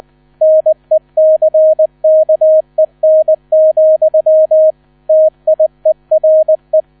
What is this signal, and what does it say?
Morse code, heard as a keyed tone, about 20 wpm, 630 Hz. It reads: NECKEN, TIERE
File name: 000 Vorwort 20wpm